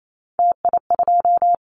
Morse code, heard as a keyed tone, about 28 wpm, 700 Hz. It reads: TI2
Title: Callsign Prefixes for Top 100 DXCC - 28wpm